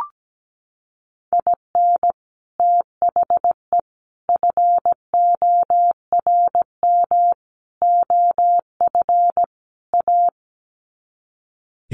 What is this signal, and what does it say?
Morse code, heard as a keyed tone, about 17 wpm, 700 Hz. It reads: IN THE FORM OF A
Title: Sets of 5 Words 17wpm